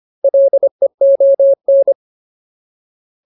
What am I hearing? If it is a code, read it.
Morse code, heard as a keyed tone, about 25 wpm, 550 Hz. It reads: LEON
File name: Ultimate Word List / 25WPM Ultimate